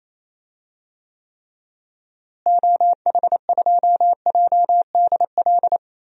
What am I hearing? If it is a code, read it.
Morse code, heard as a keyed tone, about 28 wpm, 700 Hz. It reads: OH2JDL
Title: DXCC-100 Call Signs - 6 Characters - Repeated 2x - 28wpm